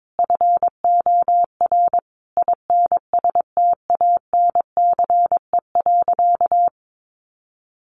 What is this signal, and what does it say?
Morse code, heard as a keyed tone, about 22 wpm, 700 Hz. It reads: FOR INSTANCE.
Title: Sentences from Top 800 Words 22wpm